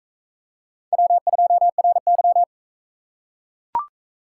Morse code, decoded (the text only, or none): W2RY